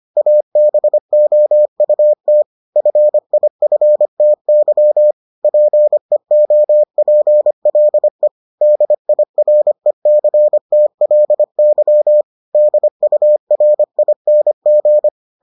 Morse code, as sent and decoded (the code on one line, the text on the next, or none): .- -... --- ..- - / ..-. .. ..-. - -.-- / .--. --- .--. .-.. / -.. .. .-. -.-. - .-.. -.-- / -.. ..- .-. .. -. --.
ABOUT FIFTY PEOPLE DIRECTLY DURING